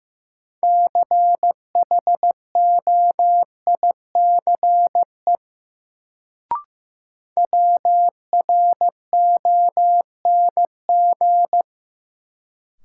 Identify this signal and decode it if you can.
Morse code, heard as a keyed tone, about 15 wpm, 700 Hz. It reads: CHOICE WRONG
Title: Sentences from Top 900 Words - New Word Drill - 15wpm